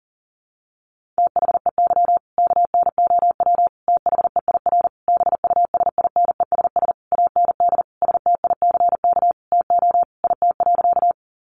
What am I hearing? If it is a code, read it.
Morse code, heard as a keyed tone, about 40 wpm, 700 Hz. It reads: THEY KNOW THEIR BUSINESS AND STICK TO IT.
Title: Sentences from Top 1200 Words 40wpm